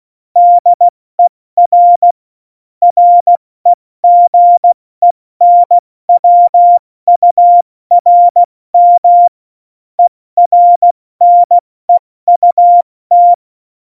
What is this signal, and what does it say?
Morse code, heard as a keyed tone, about 16 wpm, 700 Hz. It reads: DER REGENWURM ERNEUT
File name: Morsekurs / Witze / Witz 14. August 2018, 70 F80